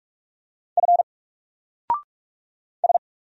Morse code, heard as a keyed tone, about 44 wpm, 700 Hz. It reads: F S
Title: Single Letter-Number 45wpm